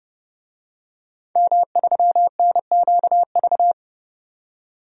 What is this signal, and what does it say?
Morse code, heard as a keyed tone, about 31 wpm, 700 Hz. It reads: M3NQV
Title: DXCC-100 Call Signs - 5 Characters - 30wpm